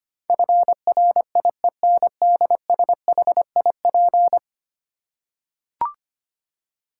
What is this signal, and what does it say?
Morse code, heard as a keyed tone, about 25 wpm, 690 Hz. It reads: FRIENDSHIP
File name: 10 Letter Words - First 3 Letters Given - 25wpm